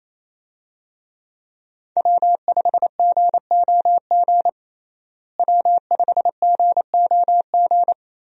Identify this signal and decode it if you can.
Morse code, heard as a keyed tone, about 28 wpm, 700 Hz. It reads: W5GOG W5GOG
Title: USA Call Signs - 5 Characters - Repeated 2x - 28wpm